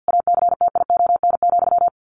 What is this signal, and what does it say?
Morse code, heard as a keyed tone, about 50 wpm, 700 Hz. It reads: ACTION,
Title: A Sound Of Thunder 50wpm